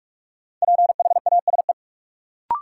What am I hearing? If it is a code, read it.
Morse code, heard as a keyed tone, about 46 wpm, 700 Hz. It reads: PHASE
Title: Sentences from Top 1200 Words - New Word Drill Alt - 45wpm